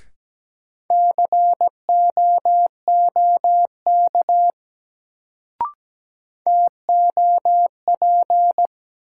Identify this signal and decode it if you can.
Morse code, heard as a keyed tone, about 17 wpm, 700 Hz. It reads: COOK TOP
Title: Sentences from Top 700 Words - New Word Drill - 17wpm